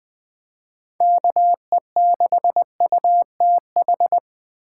Morse code, heard as a keyed tone, about 20 wpm, 700 Hz. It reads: KE6UTH
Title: USA Call Signs - 6 Characters - 6x Character Spacing - 20wpm